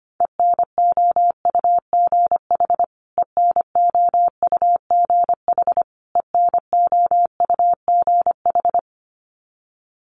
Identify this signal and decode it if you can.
Morse code, heard as a keyed tone, about 25 wpm, 700 Hz. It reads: ENOUGH ENOUGH ENOUGH
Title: Top 500 Words Repeated 25wpm